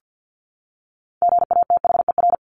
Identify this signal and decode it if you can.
Morse code, heard as a keyed tone, about 50 wpm, 700 Hz. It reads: GATHER
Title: Top 1501-1750 Words 50wpm